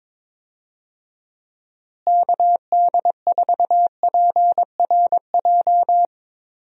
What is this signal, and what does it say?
Morse code, heard as a keyed tone, about 22 wpm, 700 Hz. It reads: KD4PRJ
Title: USA Call Signs - 6 Characters - Repeated 2x - 22wpm